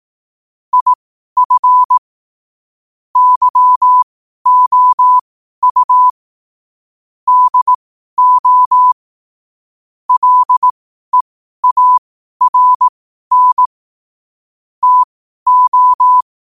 Morse code, heard as a keyed tone, about 18 wpm, spaced as for 12 wpm, 1000 Hz. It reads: IF YOU DO LEARN TO
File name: qotdfiles / 12wpm